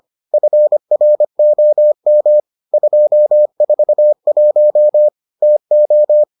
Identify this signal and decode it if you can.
Morse code, heard as a keyed tone, about 25 wpm, 600 Hz. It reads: FROM 241 TO